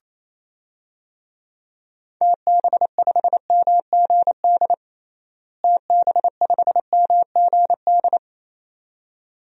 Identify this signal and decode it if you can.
Morse code, heard as a keyed tone, about 28 wpm, 700 Hz. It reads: TB5MGD TB5MGD
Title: DXCC-100 Call Signs - 6 Characters - Repeated 2x - 28wpm